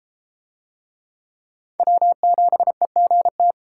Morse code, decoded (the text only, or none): W7EGT